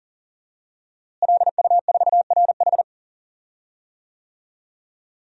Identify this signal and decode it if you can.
Morse code, heard as a keyed tone, about 40 wpm, 700 Hz. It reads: LU4RH
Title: DXCC-100 Call Signs - 5 Characters - Repeated 2x - 40wpm